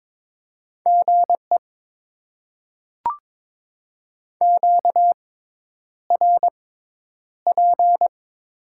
Morse code, heard as a keyed tone, about 22 wpm, spaced as for 10 wpm, 700 Hz. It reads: GE QRP
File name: Top 90 QSO Elements - 6x Character Spacing - 22wpm